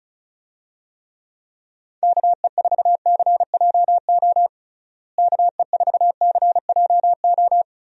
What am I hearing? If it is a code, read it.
Morse code, heard as a keyed tone, about 36 wpm, 700 Hz. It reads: KE4CJO KE4CJO
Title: USA Call Signs - 6 Characters - Repeated 2x - 35wpm